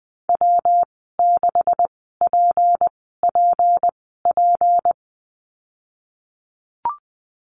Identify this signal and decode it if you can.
Morse code, heard as a keyed tone, about 20 wpm, spaced as for 14 wpm, 700 Hz. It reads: W6PPP
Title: 154 Lesson 25g - Letter p - Call Signs 20wpm